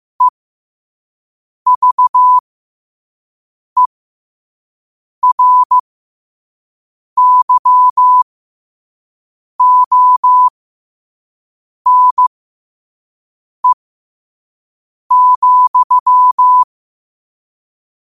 Morse code, heard as a keyed tone, about 15 wpm, spaced as for 5 wpm, 1000 Hz. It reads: EVERYONE,
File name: qotdfiles / 5wpm